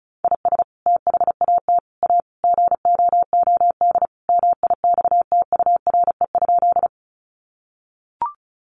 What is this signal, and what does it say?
Morse code, heard as a keyed tone, about 34 wpm, 700 Hz. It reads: IS THAT A GOOD MIXTURE?
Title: Sentences from Top 1500 Words 35wpm